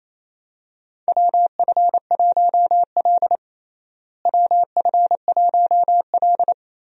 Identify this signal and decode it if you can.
Morse code, heard as a keyed tone, about 28 wpm, 700 Hz. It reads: WF1L WF1L
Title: USA Call Signs - 4 Characters - Repeated 3x - 28wpm